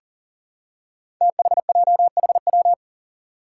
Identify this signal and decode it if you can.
Morse code, heard as a keyed tone, about 39 wpm, 700 Hz. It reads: THJHW
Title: Five Challenging Letter-Number Combinations 40wpm